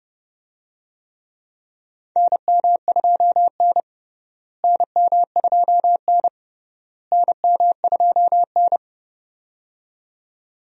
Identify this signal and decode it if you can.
Morse code, heard as a keyed tone, about 32 wpm, 700 Hz. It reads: NM2N NM2N NM2N
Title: USA Call Signs - 4 Characters - Repeated 3x - 30wpm